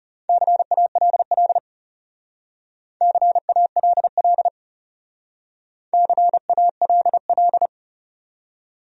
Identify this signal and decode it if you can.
Morse code, heard as a keyed tone, about 34 wpm, 700 Hz. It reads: CALL CALL CALL
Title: Speed-Racing Top 300 Words 30wpm